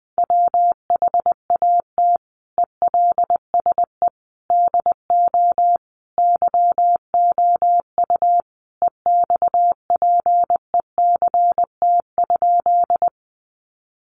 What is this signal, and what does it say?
Morse code, heard as a keyed tone, about 20 wpm, 700 Hz. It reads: WHAT ELSE DO YOU EXPECT?
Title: Sentences from Top 1100 Words 20wpm